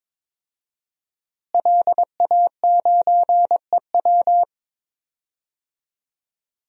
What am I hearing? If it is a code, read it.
Morse code, heard as a keyed tone, about 22 wpm, 700 Hz. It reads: LA9EW
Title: DXCC-100 Call Signs - 5 Characters - Repeated 2x - 22wpm